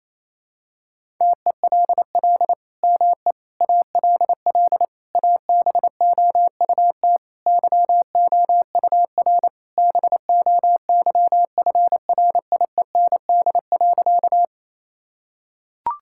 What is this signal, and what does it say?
Morse code, heard as a keyed tone, about 28 wpm, 700 Hz. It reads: TELL ME ALL ABOUT YOUR BOYFRIEND.
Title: Sentences from Top 1000 Words - Review Part 5 - 28wpm